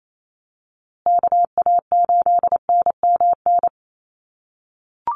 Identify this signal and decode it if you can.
Morse code, heard as a keyed tone, about 28 wpm, 690 Hz. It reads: KA8NMN
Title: USA Call Signs Repeated 3x 28wpm